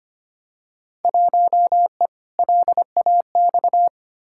Morse code, heard as a keyed tone, about 25 wpm, 700 Hz. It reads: 1E LAX
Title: Field Day Exchanges - 25wpm